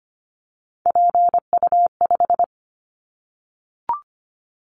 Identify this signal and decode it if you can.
Morse code, heard as a keyed tone, about 25 wpm, 700 Hz. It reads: PU5
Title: Callsign Prefixes for Top 100 DXCC - 25wpm